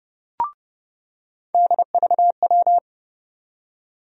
Morse code, heard as a keyed tone, about 30 wpm, 700 Hz. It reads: DVW
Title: USA Call Sign Postfixes - 3 Letters 30wpm